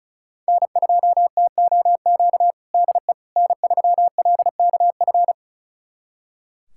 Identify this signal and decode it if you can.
Morse code, heard as a keyed tone, about 36 wpm, 700 Hz. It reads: N2TOQ DE N3LKF
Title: USA Callsign DE Callsign - 5 Characters - 35wpm